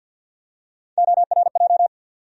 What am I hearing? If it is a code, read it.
Morse code, heard as a keyed tone, about 52 wpm, 700 Hz. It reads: ORJ
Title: USA Call Sign Postfixes - 3 Letters 50wpm